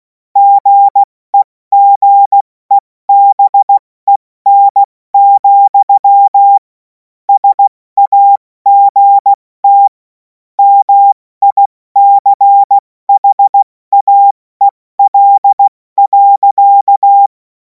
Morse code, heard as a keyed tone, about 16 wpm, 800 Hz. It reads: GEGEBEN, SAGT MICHAEL.
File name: Morsekurs / Witze / Witz 07. April 2026 70 F80